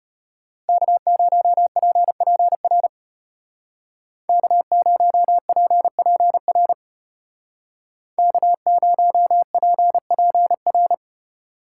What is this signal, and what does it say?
Morse code, heard as a keyed tone, about 32 wpm, 700 Hz. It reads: K0PPR K0PPR K0PPR